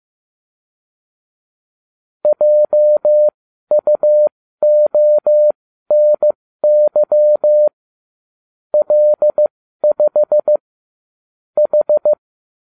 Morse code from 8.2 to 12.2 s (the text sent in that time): L5 H